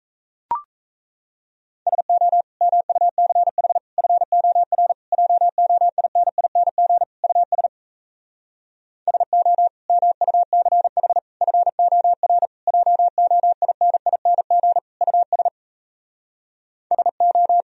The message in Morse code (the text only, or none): SO MUCH FOR JOINING US SO MUCH FOR JOINING US SO